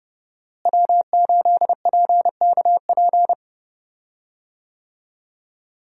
Morse code, heard as a keyed tone, about 31 wpm, 700 Hz. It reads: W8PKP
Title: USA Call Signs - 5 Characters - Repeated 2x - 30wpm